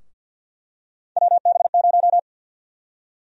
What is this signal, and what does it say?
Morse code, heard as a keyed tone, about 51 wpm, 700 Hz. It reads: WB0
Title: USA Call Sign Prefixes - 50wpm